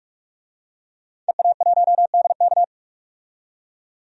Morse code, heard as a keyed tone, about 47 wpm, 700 Hz. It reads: EA1DK
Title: DXCC-100 Call Signs - 5 Characters - 45wpm